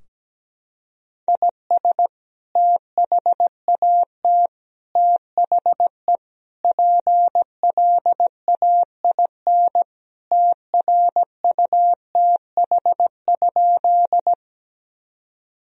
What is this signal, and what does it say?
Morse code, heard as a keyed tone, about 17 wpm, 700 Hz. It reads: IS THAT THE PLAIN TRUTH?
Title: Sentences from Top 1000 Words - Review Part 2 - 17wpm